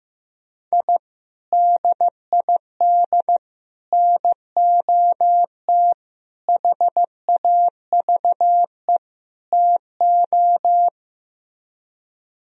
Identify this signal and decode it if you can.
Morse code, heard as a keyed tone, about 15 wpm, 700 Hz. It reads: I DID NOT HAVE TO
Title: Speed-Racing - Sets of 5 Words 15wpm